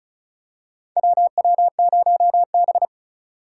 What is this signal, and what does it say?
Morse code, heard as a keyed tone, about 36 wpm, 700 Hz. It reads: WW0B
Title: USA Call Signs - 4 Characters - 4x Character Spacing - 35wpm